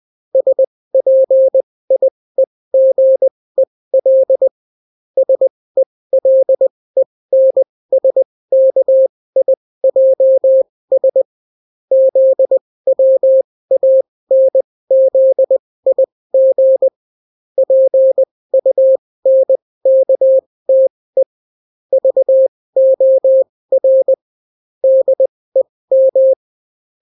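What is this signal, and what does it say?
Morse code, heard as a keyed tone, about 20 wpm, spaced as for 16 wpm, 530 Hz. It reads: SPIEGEL SELENSKIJS ZWANZIG PUNKTE VOR DEM
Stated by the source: Daily Morse weather and top stories at 16 wpm, clean without QRN.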